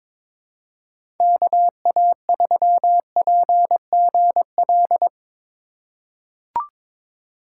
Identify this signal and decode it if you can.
Morse code, heard as a keyed tone, about 22 wpm, 700 Hz. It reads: KA3PGL
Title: USA Call Signs - 6 Characters - 4x Character Spacing - 22wpm